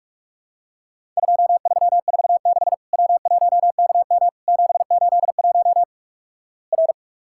Morse code, heard as a keyed tone, about 47 wpm, 700 Hz. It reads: 2346 W1KM 781 R
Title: Worked All Europe DX Contest - QTC Exchange - 45wpm